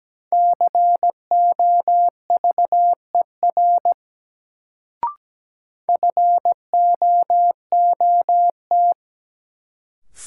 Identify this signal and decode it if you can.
Morse code, heard as a keyed tone, about 17 wpm, 700 Hz. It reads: COVER FOOT
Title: Sentences from Top 500 Words - New Word Drill - 17wpm